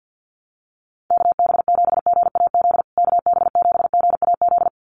Morse code, heard as a keyed tone, about 50 wpm, 700 Hz. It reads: KB7GAZ KB7GAZ
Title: USA Call Signs Repeated 2x 50wpm